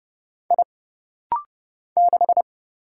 Morse code, heard as a keyed tone, about 30 wpm, 700 Hz. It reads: I 6
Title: Single Letter-Number - Rapid-Fire 30wpm